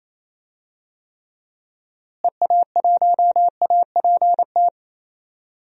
Morse code, heard as a keyed tone, about 27 wpm, 700 Hz. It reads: EA1APT
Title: DXCC-100 Call Signs - 6 Characters - Repeated 3x - 28wpm